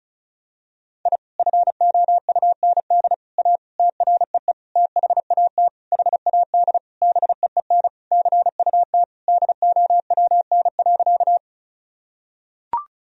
Morse code, heard as a keyed tone, about 34 wpm, 690 Hz. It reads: I FOUND A TREE THAT HAD BEEN CUT DOWN.
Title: Sentences from Top 1000 Words - Review Part 1 - 35wpm